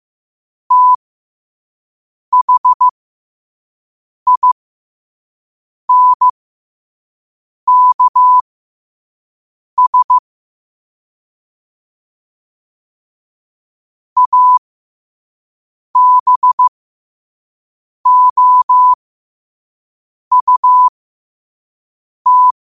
Morse code, heard as a keyed tone, about 15 wpm, spaced as for 5 wpm, 1000 Hz. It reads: THINKS ABOUT
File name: qotdfiles / 5wpm